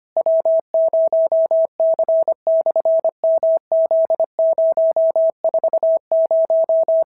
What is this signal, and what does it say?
Morse code, heard as a keyed tone, about 25 wpm, 650 Hz. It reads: W0C/MZ040
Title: SOTA QSO Practice 25wpm